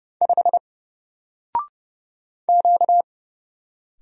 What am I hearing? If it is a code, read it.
Morse code, heard as a keyed tone, about 30 wpm, 700 Hz. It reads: 5 Q
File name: Single-Letter-Number-30wpm